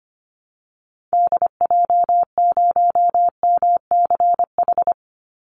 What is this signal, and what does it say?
Morse code, heard as a keyed tone, about 25 wpm, 700 Hz. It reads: DJ0MCH
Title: Call Signs Repeated 2x - 25wpm